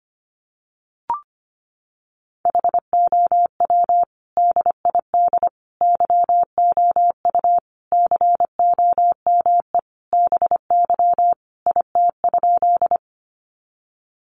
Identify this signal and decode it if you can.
Morse code, heard as a keyed tone, about 25 wpm, 700 Hz. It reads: HOW DID YOU COME BY IT?
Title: Sentences from Top 100 Words 25wpm